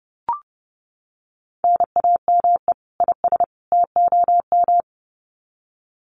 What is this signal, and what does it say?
Morse code, heard as a keyed tone, about 30 wpm, 700 Hz. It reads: NAME IS TOM
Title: Common QSO Phrases 30wpm